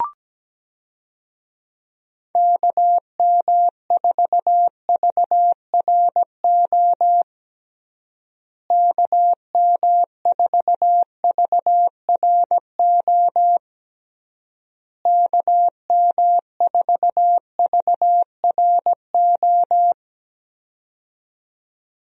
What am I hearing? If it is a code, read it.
Morse code, heard as a keyed tone, about 17 wpm, 700 Hz. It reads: KM4VRO KM4VRO KM4VRO
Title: USA Call Signs - 6 Characters - Repeated 3x - 17wpm